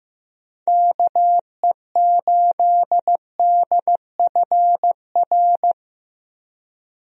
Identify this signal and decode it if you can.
Morse code, heard as a keyed tone, about 15 wpm, 700 Hz. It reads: KE8DFR
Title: USA Call Signs - 6 Characters - Repeated 3x - 15wpm